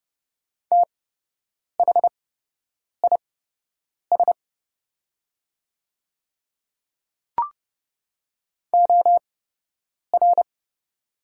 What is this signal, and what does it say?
Morse code, heard as a keyed tone, about 30 wpm, spaced as for 8 wpm, 700 Hz. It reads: THIS OR
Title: 8x Character Spacing - Top 100 Words - 30wpm